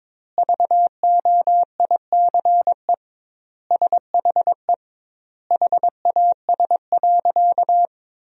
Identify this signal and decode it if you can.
Morse code, heard as a keyed tone, about 22 wpm, 700 Hz. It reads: VOICE SHE HAS.
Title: 2x Word-Spacing - Sentences from Top 500 Words - 22wpm